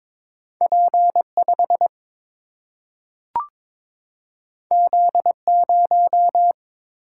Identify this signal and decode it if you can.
Morse code, heard as a keyed tone, about 22 wpm, 700 Hz. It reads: P 5 Z 0
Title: Single-Letter Single-Number 22wpm